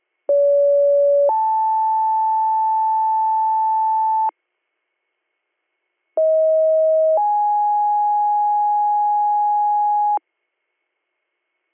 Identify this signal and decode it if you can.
Two-tone page, heard as a one-second tone, then a three-second tone, about 570 then 880 Hz, about 640 then 830 Hz. It is heard from 0.3 to 10.2 s.